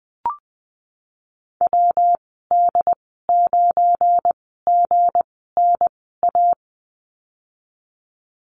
Morse code, heard as a keyed tone, about 20 wpm, spaced as for 15 wpm, 700 Hz. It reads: WD9GNA
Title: 186 Lesson 30g - Letter b - Call Signs 20wpm